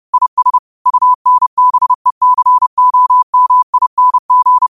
Morse code, heard as a keyed tone, about 30 wpm, 1000 Hz. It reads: IS UNBECOMING
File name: qotdfiles / 30wpm